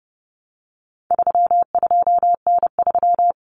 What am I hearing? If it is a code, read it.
Morse code, heard as a keyed tone, about 30 wpm, 700 Hz. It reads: 32N3
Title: Common Cut Numbers - 4-Digit 30wpm